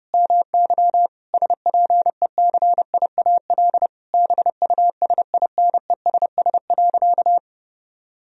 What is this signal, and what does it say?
Morse code, heard as a keyed tone, about 30 wpm, 700 Hz. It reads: MY SPECIAL BUSINESS.
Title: Sentences from Top 1000 Words - Review Part 3 - 30wpm